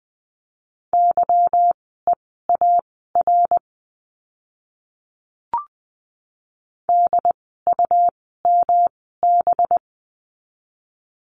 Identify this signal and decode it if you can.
Morse code, heard as a keyed tone, about 20 wpm, spaced as for 14 wpm, 700 Hz. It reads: YEAR DUMB
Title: 185 Lesson 30f - Letter b - Words 20wpm